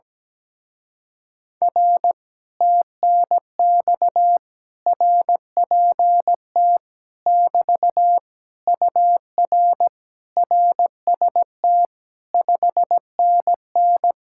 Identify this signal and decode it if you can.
Morse code, heard as a keyed tone, about 17 wpm, 700 Hz. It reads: R TNX RPT <BT> UR RST 5NN
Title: Speed-Racing Common QSO Phrases - 17wpm